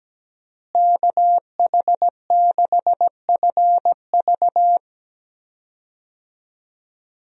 Morse code, heard as a keyed tone, about 17 wpm, 700 Hz. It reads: KH6FV
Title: USA Call Signs - 5 Characters - Repeated 2x - 17wpm